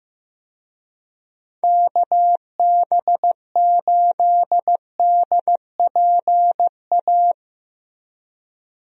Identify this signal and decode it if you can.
Morse code, heard as a keyed tone, about 15 wpm, 700 Hz. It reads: KB8DPA
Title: USA Call Signs - 6 Characters - Repeated 2x - 15wpm